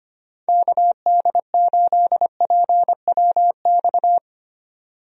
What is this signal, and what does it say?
Morse code, heard as a keyed tone, about 25 wpm, 700 Hz. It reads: KD8PWX
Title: USA Call Signs - 6 Characters - 25wpm